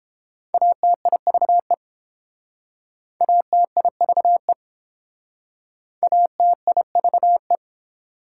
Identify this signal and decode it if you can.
Morse code, heard as a keyed tone, about 30 wpm, 700 Hz. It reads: ATIVE ATIVE ATIVE